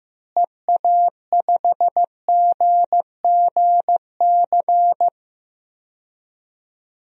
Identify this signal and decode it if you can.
Morse code, heard as a keyed tone, about 16 wpm, 700 Hz. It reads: EA5GGC
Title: DXCC-100 Call Signs - 6 Characters - Repeated 2x - 15wpm